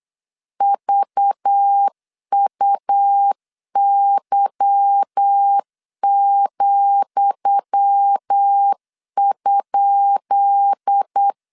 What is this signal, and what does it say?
Morse code, heard as a keyed tone, about 8 wpm, 800 Hz. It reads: VUY,?
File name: LEZIONI / 10 SIGNS / 10D COMA 8wpm 3749